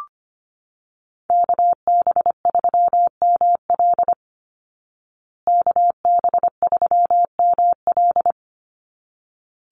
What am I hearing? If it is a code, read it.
Morse code, heard as a keyed tone, about 25 wpm, 700 Hz. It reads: KB3ML KB3ML
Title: Call Signs Repeated 3x - 25wpm